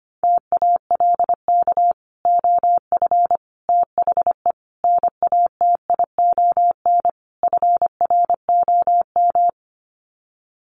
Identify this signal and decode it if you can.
Morse code, heard as a keyed tone, about 25 wpm, 700 Hz. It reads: TALK OF THE NATION FROM
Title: Sets of 5 Words 25wpm